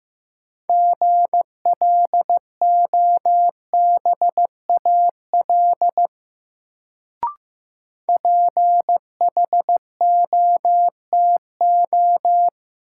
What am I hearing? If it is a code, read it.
Morse code, heard as a keyed tone, about 15 wpm, 700 Hz. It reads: GLOBAL PHOTO
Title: Sentences from Top 1200 Words - New Word Drill - 15wpm